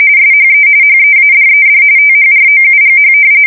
RTTY, about 45.45 baud. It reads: SHE HEADED THROUGH
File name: fsk170 / battleshipnj